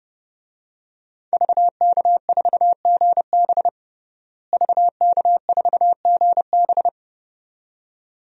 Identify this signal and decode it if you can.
Morse code, heard as a keyed tone, about 32 wpm, 700 Hz. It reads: VK4GB VK4GB
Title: DXCC-100 Call Signs - 5 Characters - Repeated 2x - 30wpm